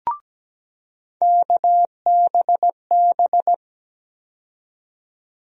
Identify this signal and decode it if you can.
Morse code, heard as a keyed tone, about 17 wpm, 700 Hz. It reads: KBB
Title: USA Call Sign Postfixes - 3 Letters 17wpm